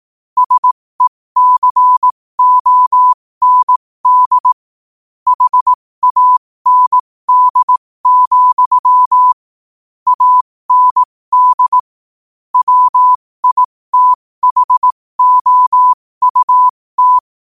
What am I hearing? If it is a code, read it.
Morse code, heard as a keyed tone, about 18 wpm, spaced as for 15 wpm, 1000 Hz. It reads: SECOND HAND, AND WITHOUT
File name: qotdfiles / 15wpm